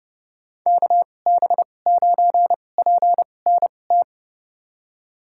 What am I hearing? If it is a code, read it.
Morse code, heard as a keyed tone, about 30 wpm, spaced as for 22 wpm, 700 Hz. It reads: KB9PNT
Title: USA Call Signs - 2x Character Spacing - 30wpm